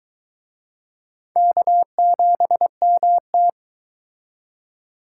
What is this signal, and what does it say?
Morse code, heard as a keyed tone, about 23 wpm, 700 Hz. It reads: K7MT